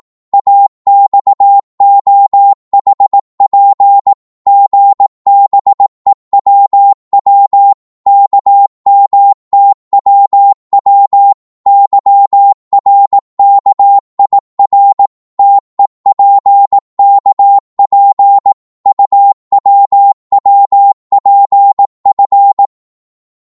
Morse code, heard as a keyed tone, about 18 wpm, 800 Hz. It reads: AXOHP GBEWW KMTWW YRKIR TEPKP UWWPF
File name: Morse / ltr-18wpm